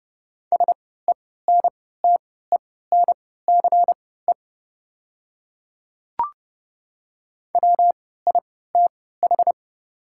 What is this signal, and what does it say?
Morse code, heard as a keyed tone, about 30 wpm, spaced as for 17 wpm, 700 Hz. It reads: SENTENCE WITH
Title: 3x Character Spacing - Top 500 Words - 30wpm